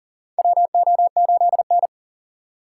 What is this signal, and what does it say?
Morse code, heard as a keyed tone, about 42 wpm, 690 Hz. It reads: WO8N
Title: USA Call Signs - 4 Characters - 2x Character Spacing - 40wpm